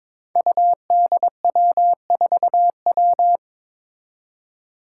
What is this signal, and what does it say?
Morse code, heard as a keyed tone, about 22 wpm, 700 Hz. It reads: UDW4W
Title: Five Challenging Letter-Number Combinations 22wpm